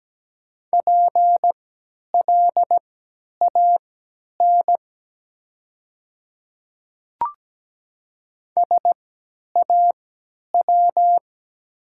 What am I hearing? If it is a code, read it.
Morse code, heard as a keyed tone, about 17 wpm, spaced as for 10 wpm, 700 Hz. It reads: PLAN SAW
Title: 3x Character Spacing - Top 500 Words - 17wpm